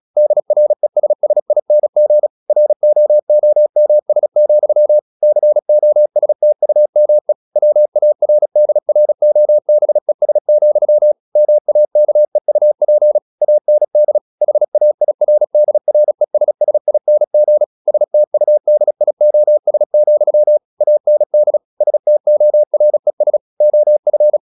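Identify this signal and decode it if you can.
Morse code, heard as a keyed tone, about 36 wpm, 600 Hz. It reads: DRESSING ROOMS, COSTUME WARDROBES, MAKEUP AND HAIRDRESSING STUDIOS, AND STORES OF
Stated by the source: Morse practice at 36 wpm, text of Ken Follett's novel Never, from CW med Gnister.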